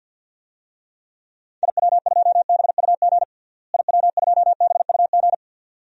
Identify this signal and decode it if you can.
Morse code, heard as a keyed tone, about 53 wpm, 700 Hz. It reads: IW2BUG IW2BUG
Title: DXCC-100 Call Signs - 6 Characters - Repeated 2x - 50wpm